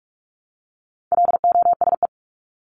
Morse code, heard as a keyed tone, about 44 wpm, 700 Hz. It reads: LOSE
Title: Top 1001-1250 Words 45wpm